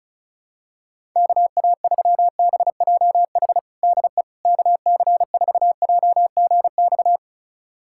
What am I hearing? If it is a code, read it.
Morse code, heard as a keyed tone, about 34 wpm, 700 Hz. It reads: KA3BJH DE KC4JGX
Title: USA Callsign DE Callsign - 6 Characters - 35wpm